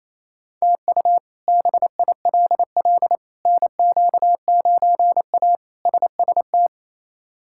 Morse code, heard as a keyed tone, about 28 wpm, 700 Hz. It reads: TU BILL NQ9A SST
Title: K1USN SST QSO Practice - 28wpm